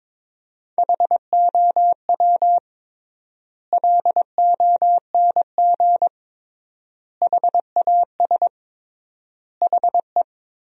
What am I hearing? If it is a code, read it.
Morse code, heard as a keyed tone, about 21 wpm, 700 Hz. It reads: HOW LONG HAS HE
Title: 3x Word-Spacing - Sentences from Top 100 Words - 22wpm